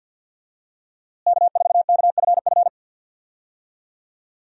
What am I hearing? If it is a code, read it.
Morse code, heard as a keyed tone, about 52 wpm, 700 Hz. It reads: K4KFF
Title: USA Call Signs - 5 Characters - Repeated 3x - 50wpm